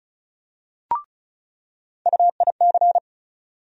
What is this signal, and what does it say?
Morse code, heard as a keyed tone, about 34 wpm, 700 Hz. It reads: UIC
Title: USA Call Sign Postfixes - 3 Letters 35wpm